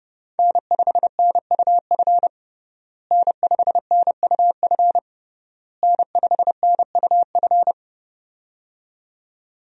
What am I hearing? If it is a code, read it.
Morse code, heard as a keyed tone, about 32 wpm, 700 Hz. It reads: N5NUF N5NUF N5NUF
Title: USA Call Signs - 5 Characters - Repeated 3x - 30wpm